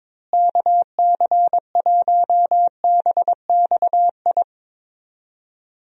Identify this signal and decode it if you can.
Morse code, heard as a keyed tone, about 22 wpm, 700 Hz. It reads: KC1BXI
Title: USA Call Signs - 6 Characters - Repeated 2x - 22wpm